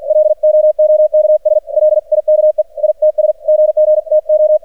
Morse code, heard as a keyed tone, about 48 wpm, 610 Hz. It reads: LOOKS LIKE ITS GG TO
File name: archived / CRSnet